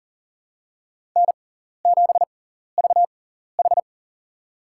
Morse code, heard as a keyed tone, about 40 wpm, spaced as for 14 wpm, 700 Hz. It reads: N7VH
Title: USA Call Signs - 6x Character Spacing - 40wpm